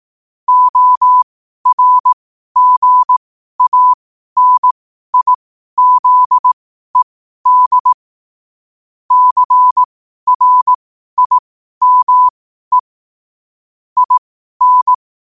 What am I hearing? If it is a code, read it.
Morse code, heard as a keyed tone, about 18 wpm, spaced as for 12 wpm, 1000 Hz. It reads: ORGANIZED CRIME IN
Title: 12 WPM morse code quotes for Fri, 01 Mar 2024